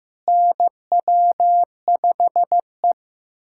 Morse code, heard as a keyed tone, about 15 wpm, 700 Hz. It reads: NW5E
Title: USA Callsign DE Callsign - 4 Characters - 15wpm